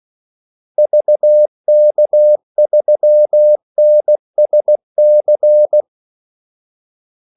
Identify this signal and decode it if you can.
Morse code, heard as a keyed tone, about 16 wpm, 600 Hz. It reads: VK3NSC
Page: Kaldesignaler 16 wpm